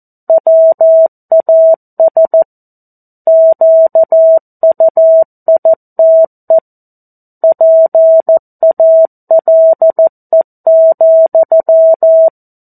Morse code, heard as a keyed tone, about 14 wpm, 650 Hz. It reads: WAS QUITE PALE,
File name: 02-Chapter-02 14 WPM